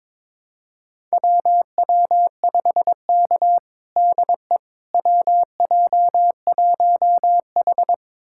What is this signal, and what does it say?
Morse code, heard as a keyed tone, about 22 wpm, 700 Hz. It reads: WW5K DE WJ1H
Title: USA Callsign DE Callsign - 4 Characters - 22wpm